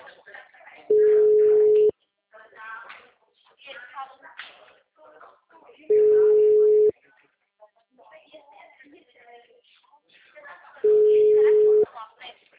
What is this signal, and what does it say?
Ringback tone (Europe, 425 Hz), with speech.
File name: out- -2010- - - .28109